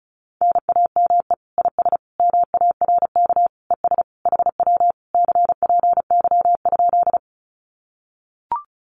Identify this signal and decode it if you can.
Morse code, heard as a keyed tone, about 36 wpm, 700 Hz. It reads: NAME IS MARK ES HW CPY?
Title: Common QSO Phrases Encore 35wpm